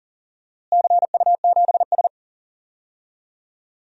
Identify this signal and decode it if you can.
Morse code, heard as a keyed tone, about 40 wpm, 690 Hz. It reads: CU7S
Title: DXCC-100 Call Signs - 4 Characters - Repeated 2x - 40wpm